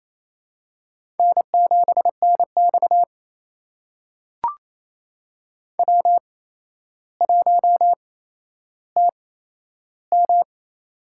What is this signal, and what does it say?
Morse code, heard as a keyed tone, about 28 wpm, spaced as for 12 wpm, 700 Hz. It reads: N7NX W1TM
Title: USA Call Signs - 4 Characters - 8x Character Spacing - 28wpm